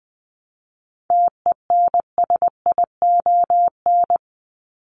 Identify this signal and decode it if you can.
Morse code, heard as a keyed tone, about 20 wpm, 700 Hz. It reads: TENSION
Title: Top 1501-1750 Words 20wpm